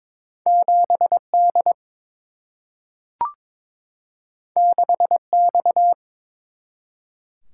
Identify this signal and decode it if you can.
Morse code, heard as a keyed tone, about 22 wpm, 700 Hz. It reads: 7 D 6 X
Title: Single-Letter Single-Number 22wpm